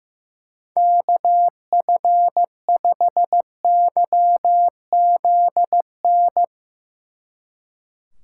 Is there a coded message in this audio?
Yes: Morse code from 0.8 to 6.4 s, heard as a keyed tone, about 19 wpm, 700 Hz. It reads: KF5YZN